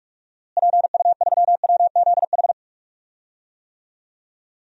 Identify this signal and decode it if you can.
Morse code, heard as a keyed tone, about 47 wpm, 700 Hz. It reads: PU3WZH
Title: DXCC-100 Call Signs - 6 Characters - Repeated 2x - 45wpm